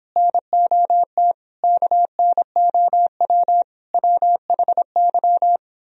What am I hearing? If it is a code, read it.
Morse code, heard as a keyed tone, about 25 wpm, 700 Hz. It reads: NOT KNOW WHY